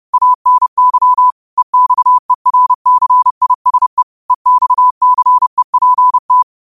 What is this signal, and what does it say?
Morse code, heard as a keyed tone, about 30 wpm, 1000 Hz. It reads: ANY EXERCISE EXCEPT
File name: qotdfiles / 30wpm